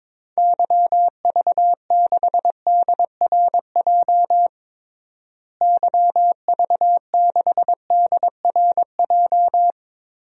Morse code, heard as a keyed tone, about 22 wpm, 700 Hz. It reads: YV6DRJ YV6DRJ
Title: DXCC-100 Call Signs - 6 Characters - Repeated 2x - 22wpm